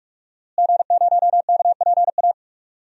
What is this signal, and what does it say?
Morse code, heard as a keyed tone, about 46 wpm, 700 Hz. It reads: G0KPA
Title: DXCC-100 Call Signs - 5 Characters - Repeated 2x - 45wpm